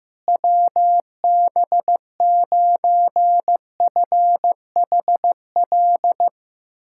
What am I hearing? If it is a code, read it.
Morse code, heard as a keyed tone, about 15 wpm, 700 Hz. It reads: WB9FHL
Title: USA Callsign DE Callsign - 6 Characters - 15wpm